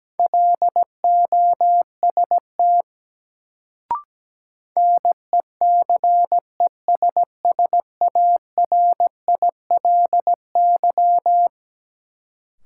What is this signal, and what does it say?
Morse code, heard as a keyed tone, about 17 wpm, 700 Hz. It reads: LOST NECESSARILY
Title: Sentences from Top 1400 Words - New Word Drill Alt - 17wpm